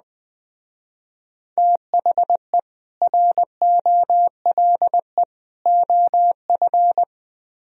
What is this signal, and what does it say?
Morse code, heard as a keyed tone, about 20 wpm, 700 Hz. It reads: THE ROLE OF